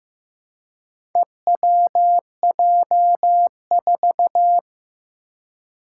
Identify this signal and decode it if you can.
Morse code, heard as a keyed tone, about 15 wpm, 700 Hz. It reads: EWJ4
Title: Four Challenging Letter-Number Combinations 15wpm